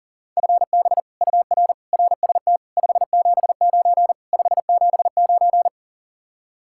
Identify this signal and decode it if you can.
Morse code, heard as a keyed tone, about 52 wpm, 700 Hz. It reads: FB UR RST 579 579